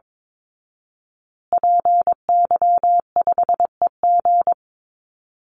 Morse code, heard as a keyed tone, about 22 wpm, 700 Hz. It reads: PY5EG
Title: Call Signs Repeated 2x - 22wpm